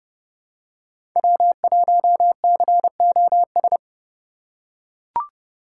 Morse code, heard as a keyed tone, about 30 wpm, 700 Hz. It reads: W1COS
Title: USA Call Signs - 5 Characters - 2x Character Spacing - 30wpm